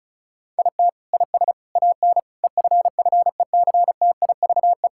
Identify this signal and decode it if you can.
Morse code, heard as a keyed tone, about 34 wpm, 700 Hz. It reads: IT IS AN EFFECTIVE
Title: Sentences from Top 1000 Words - Review Part 6 - 35wpm